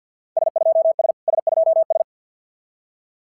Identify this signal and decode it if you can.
Morse code, heard as a keyed tone, about 50 wpm, 650 Hz. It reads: S2S S2S
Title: SOTA QSO Practice 50wpm